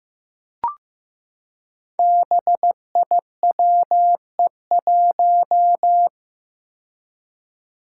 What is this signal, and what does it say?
Morse code, heard as a keyed tone, about 15 wpm, 700 Hz. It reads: BIWE1
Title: Five Challenging Letter-Number Combinations 15wpm